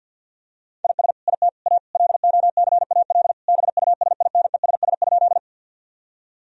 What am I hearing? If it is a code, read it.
Morse code, heard as a keyed tone, about 50 wpm, 700 Hz. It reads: IS IT A LOCAL BUSINESS?
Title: Sentences from Top 1000 Words - Review Part 3 - 50wpm